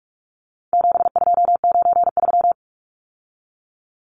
Morse code, heard as a keyed tone, about 44 wpm, 700 Hz. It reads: 7293
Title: Numbers - 4-Digit 45wpm